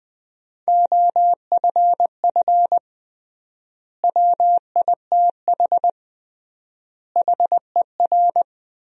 Morse code, heard as a keyed tone, about 20 wpm, 700 Hz. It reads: OFF WITH HER
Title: 3x Word-Spacing - Sentences from Top 300 Words - 20wpm